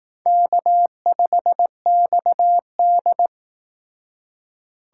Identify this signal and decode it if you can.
Morse code, heard as a keyed tone, about 18 wpm, 700 Hz. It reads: K5XD
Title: Speed Racing - USA Call Signs 15wpm